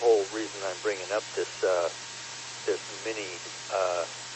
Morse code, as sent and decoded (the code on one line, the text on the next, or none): none